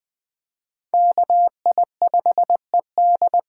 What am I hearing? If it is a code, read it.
Morse code, heard as a keyed tone, about 20 wpm, 700 Hz. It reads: KI5ED
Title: USA Call Signs - 6 Characters - 6x Character Spacing - 20wpm